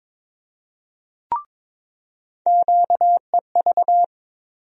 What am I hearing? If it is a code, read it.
Morse code, heard as a keyed tone, about 22 wpm, 700 Hz. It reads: QEV
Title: USA Call Sign Postfixes - 3 Letters 22wpm